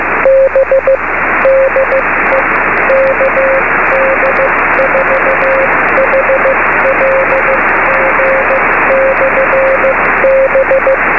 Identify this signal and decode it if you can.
Morse code, heard as a keyed tone, about 15 wpm, spaced as for 13 wpm, 540 Hz, with radio noise. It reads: BDE KD4HLG/B
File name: beacons / kd4hlg-b